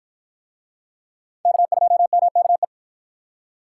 Morse code, heard as a keyed tone, about 52 wpm, 700 Hz. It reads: K2MKE